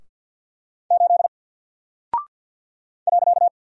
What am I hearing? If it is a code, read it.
Morse code, heard as a keyed tone, about 48 wpm, 700 Hz. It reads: 8 .
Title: Single Character (Letter:Number:Punctuation:Prosign) 50wpm